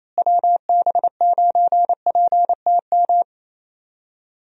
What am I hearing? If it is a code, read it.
Morse code, heard as a keyed tone, about 28 wpm, 700 Hz. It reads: WB9PTM
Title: USA Call Signs - 6 Characters - Repeated 3x - 28wpm